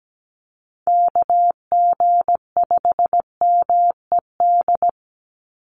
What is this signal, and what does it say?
Morse code, heard as a keyed tone, about 17 wpm, 700 Hz. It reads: KG5MED
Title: USA Call Signs 17wpm